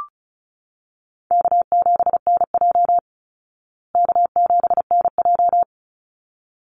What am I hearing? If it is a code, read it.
Morse code, heard as a keyed tone, about 34 wpm, 700 Hz. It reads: K7NJ K7NJ
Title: Call Signs Repeated 3x - 35wpm